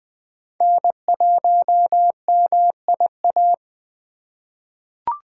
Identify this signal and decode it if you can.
Morse code, heard as a keyed tone, about 21 wpm, 700 Hz. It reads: N1MIA
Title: USA Call Signs - 5 Characters - 4x Character Spacing - 20wpm